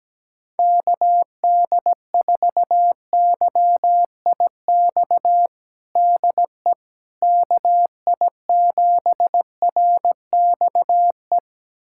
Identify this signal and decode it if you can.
Morse code, heard as a keyed tone, about 17 wpm, 700 Hz. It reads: KD4YIX DE KI7RXE
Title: USA Callsign DE Callsign - 6 Characters - 17wpm